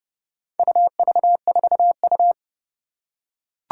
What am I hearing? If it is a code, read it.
Morse code, heard as a keyed tone, about 30 wpm, 690 Hz. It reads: UV4U
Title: Four Challenging Letter-Number Combinations 30wpm